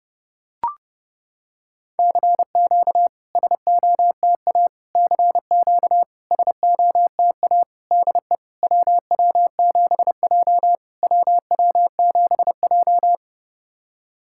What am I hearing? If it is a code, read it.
Morse code, heard as a keyed tone, about 30 wpm, 700 Hz. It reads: CQ SOTA CQ SOTA DE WW7J WW7J
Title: SOTA QSO Practice 30wpm